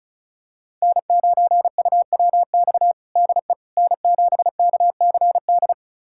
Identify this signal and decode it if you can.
Morse code, heard as a keyed tone, about 34 wpm, 700 Hz. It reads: N9UWX DE N7KCD
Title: USA Callsign DE Callsign - 5 Characters - 35wpm